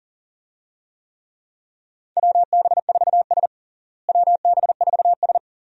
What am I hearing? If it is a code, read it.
Morse code, heard as a keyed tone, about 40 wpm, 700 Hz. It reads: WB4S WB4S
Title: USA Call Signs - 4 Characters - Repeated 2x - 40wpm